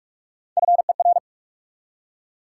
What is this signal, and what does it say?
Morse code, heard as a keyed tone, about 47 wpm, 700 Hz. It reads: FER
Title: Top 180 QSO Elements 45wpm